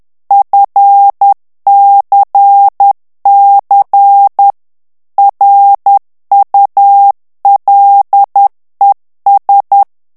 Morse code, heard as a keyed tone, about 10 wpm, 800 Hz. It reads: FCC RULES
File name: ANGLAIS texte clair 3 vitesse 10 mots minute